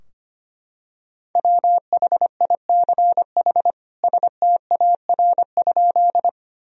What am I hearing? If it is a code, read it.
Morse code, heard as a keyed tone, about 25 wpm, 700 Hz. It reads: WHICH STAR?
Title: Sentences from Top 1000 Words - Review Part 2 - 25wpm